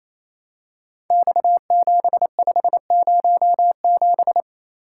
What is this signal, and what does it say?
Morse code, heard as a keyed tone, about 28 wpm, 700 Hz. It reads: X7507
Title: Code Groups - 5 Letter:Numbers - 28wpm